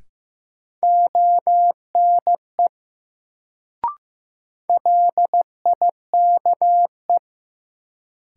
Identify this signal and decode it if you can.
Morse code, heard as a keyed tone, about 15 wpm, 700 Hz. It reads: ONE LIKE
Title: Sentences from Top 100 Words - New Word Drill - 15wpm